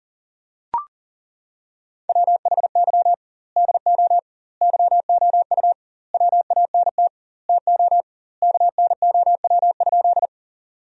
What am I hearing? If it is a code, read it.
Morse code, heard as a keyed tone, about 40 wpm, 700 Hz. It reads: WHY DO YOU WANT TO KNOW?
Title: 2x Word-Spacing - Sentences from Top 200 Words - 40wpm